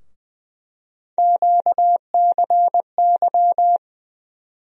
Morse code, heard as a keyed tone, about 20 wpm, 700 Hz. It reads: QCY
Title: USA Call Sign Postfixes - 3 Letters 20wpm